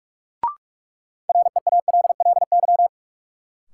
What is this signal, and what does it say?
Morse code, heard as a keyed tone, about 46 wpm, 700 Hz. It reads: REALLY
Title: Sentences from Top 600 Words - New Word Drill - 45wpm